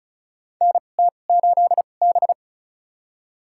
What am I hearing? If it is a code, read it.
Morse code, heard as a keyed tone, about 34 wpm, spaced as for 25 wpm, 700 Hz. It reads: NT8B
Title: USA Call Signs - 2x Character Spacing - 35wpm